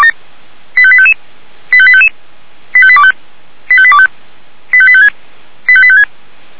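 Five-tone selective call (ZVEI), heard as quick runs of five tones, tones of about 70 ms. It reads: 85855 85855 85825 85825 85856 85856
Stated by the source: ZVEI five-tone selective call 85728 received by SDR.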